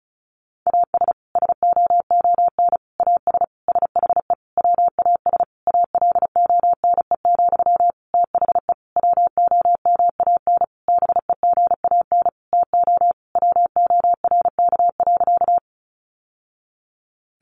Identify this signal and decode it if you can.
Morse code, heard as a keyed tone, about 34 wpm, 700 Hz. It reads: AS SOON AS SHE WAS ALONE, THE WOMAN BEGAN TO WORK.
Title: Sentences from Top 1100 Words 35wpm